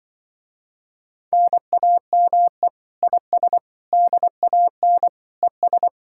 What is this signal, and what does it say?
Morse code, heard as a keyed tone, about 25 wpm, 700 Hz. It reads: NAME IS DAN ES